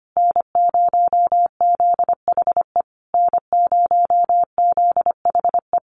Morse code, heard as a keyed tone, about 25 wpm, 700 Hz. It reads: N0ZHE N0ZHE
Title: Common QSO Phrases Encore 25wpm